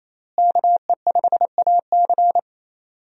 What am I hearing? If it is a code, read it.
Morse code, heard as a keyed tone, about 28 wpm, 700 Hz. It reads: KE5AC
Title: USA Call Signs - 5 Characters - Repeated 2x - 28wpm